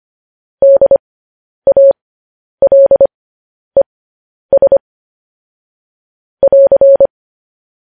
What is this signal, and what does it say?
Morse code, heard as a keyed tone, about 25 wpm, spaced as for 10 wpm, 550 Hz. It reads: DALES <AR>
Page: news headlines in morse code at 10 wpm